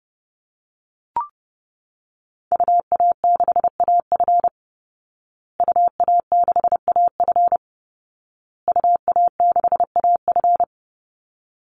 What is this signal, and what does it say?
Morse code, heard as a keyed tone, about 30 wpm, 700 Hz. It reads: UA6AF UA6AF UA6AF
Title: Call Signs Repeated 2x - 30wpm